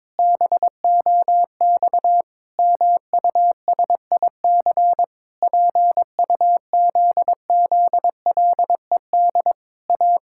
Morse code, heard as a keyed tone, about 22 wpm, 700 Hz. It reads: BOX MUSIC PUZZLED A
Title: Pangram Challenge v2 - 22wpm